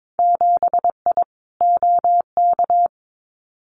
Morse code, heard as a keyed tone, about 22 wpm, 700 Hz. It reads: 7I OK
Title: Winter Field Day Exchanges Encore 22wpm